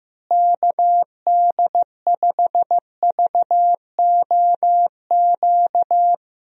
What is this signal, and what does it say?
Morse code, heard as a keyed tone, about 15 wpm, 700 Hz. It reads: KD5VOQ
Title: USA Call Signs - 6 Characters - Repeated 2x - 15wpm